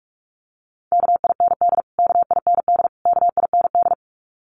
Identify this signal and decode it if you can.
Morse code, heard as a keyed tone, about 44 wpm, 700 Hz. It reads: KIND KIND KIND
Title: Top 300 Words Repeated 45wpm